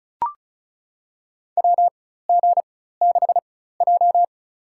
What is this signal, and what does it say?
Morse code, heard as a keyed tone, about 34 wpm, spaced as for 16 wpm, 700 Hz. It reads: WG6J
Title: USA Call Signs - 4x Character Spacing - 35wpm